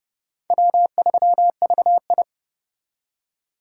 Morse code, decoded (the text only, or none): W3VI